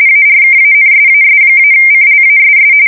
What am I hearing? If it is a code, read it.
RTTY, about 45.45 baud. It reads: ARKANSAS-CLASS
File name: fsk170 / usnavy3